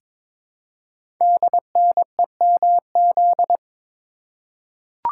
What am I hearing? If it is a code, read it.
Morse code, heard as a keyed tone, about 23 wpm, 700 Hz. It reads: DNEMZ
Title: Code Groups - 5 Letters - 22wpm